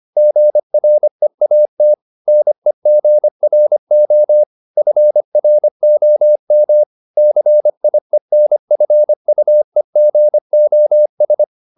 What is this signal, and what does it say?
Morse code, heard as a keyed tone, about 25 wpm, 600 Hz. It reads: GREAT NEGRO FROM CIENFUEGOS
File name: the old man and the sea part 1 25wpm